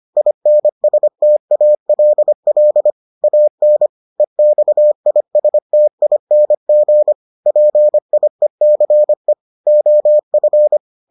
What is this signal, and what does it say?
Morse code, heard as a keyed tone, about 25 wpm, 600 Hz. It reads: INSTALL AN EXISTING PIECE OF